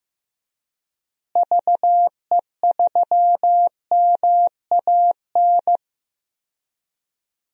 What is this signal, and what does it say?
Morse code, heard as a keyed tone, about 15 wpm, 700 Hz. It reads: VE3MAN
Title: DXCC-100 Call Signs - 6 Characters - Repeated 2x - 15wpm